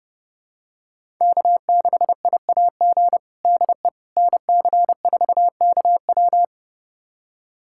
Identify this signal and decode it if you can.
Morse code, heard as a keyed tone, about 32 wpm, 700 Hz. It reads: K6IAG DE NC4KW
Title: USA Callsign DE Callsign - 5 Characters - 30wpm